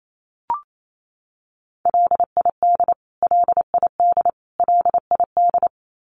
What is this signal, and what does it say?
Morse code, heard as a keyed tone, about 28 wpm, 700 Hz. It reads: LID LID LID
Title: Abbreviations Repeated 3x - 28wpm